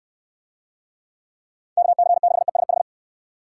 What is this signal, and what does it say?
Morse code, heard as a keyed tone, about 69 wpm, 700 Hz. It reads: KC6ESH